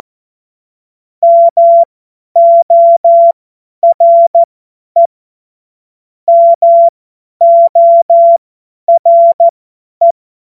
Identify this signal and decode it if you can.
Morse code, heard as a keyed tone, about 14 wpm, spaced as for 10 wpm, 680 Hz. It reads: MORE MORE
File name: Morsekurs / Witze / Witz 14. Dezember 2021, 50 F70